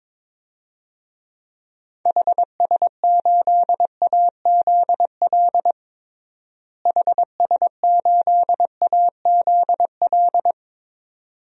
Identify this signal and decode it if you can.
Morse code, heard as a keyed tone, about 22 wpm, 700 Hz. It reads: HS8AZL HS8AZL
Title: DXCC-100 Call Signs - 6 Characters - Repeated 2x - 22wpm